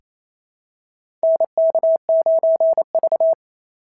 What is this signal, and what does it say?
Morse code, heard as a keyed tone, about 28 wpm, 690 Hz. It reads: NK9V
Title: SOTA QSO Practice 28wpm